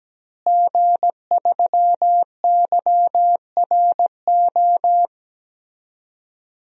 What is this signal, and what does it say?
Morse code, heard as a keyed tone, about 17 wpm, 700 Hz. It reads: G3YRO
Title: DXCC-100 Call Signs - 5 Characters - Repeated 3x - 17wpm